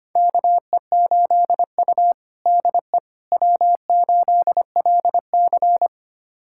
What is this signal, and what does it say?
Morse code, heard as a keyed tone, about 25 wpm, 700 Hz. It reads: KE8U DE W8LC
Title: USA Callsign DE Callsign - 4 Characters - 25wpm